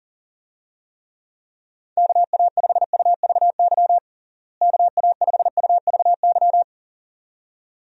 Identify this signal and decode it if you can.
Morse code, heard as a keyed tone, about 37 wpm, 700 Hz. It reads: KA5UVY KA5UVY
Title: USA Call Signs - 6 Characters - Repeated 2x - 40wpm